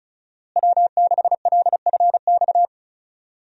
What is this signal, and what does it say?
Morse code, heard as a keyed tone, about 36 wpm, 700 Hz. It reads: W6LFX
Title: USA Call Signs - 5 Characters - Repeated 2x - 35wpm